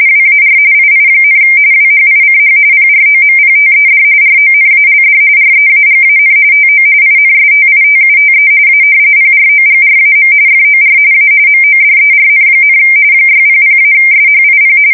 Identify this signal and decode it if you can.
RTTY, about 45.45 baud. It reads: WAR II, WHICH PROMPTED RONALD REAGAN TO INSTITUTE A PROGRAM FOR A MODERN, 600-SHIP
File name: fsk170 / usnavy1